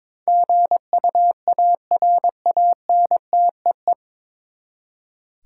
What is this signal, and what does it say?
Morse code, heard as a keyed tone, about 22 wpm, 700 Hz. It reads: GUARANTEE
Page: Morse Code Ninja practice 22wpm